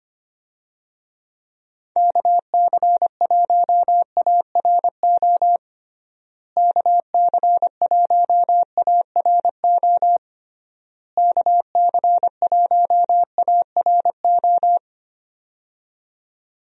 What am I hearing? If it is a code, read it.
Morse code, heard as a keyed tone, about 25 wpm, 700 Hz. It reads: KC1ARO KC1ARO KC1ARO
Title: USA Call Signs - 6 Characters - Repeated 3x - 25wpm